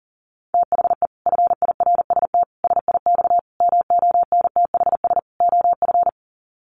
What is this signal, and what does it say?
Morse code, heard as a keyed tone, about 40 wpm, 700 Hz. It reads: THE FIRST SIX MONTHS OF
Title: Sets of 5 Words Encore2 40wpm